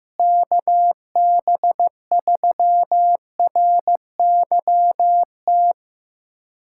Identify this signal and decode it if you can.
Morse code, heard as a keyed tone, about 15 wpm, 700 Hz. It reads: KB3RYT
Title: USA Callsign DE Callsign - 6 Characters - 15wpm